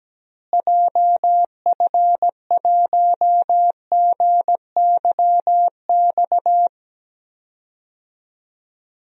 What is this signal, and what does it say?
Morse code, heard as a keyed tone, about 17 wpm, 700 Hz. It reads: JF1GYX
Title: DXCC-100 Call Signs - 6 Characters - Repeated 3x - 17wpm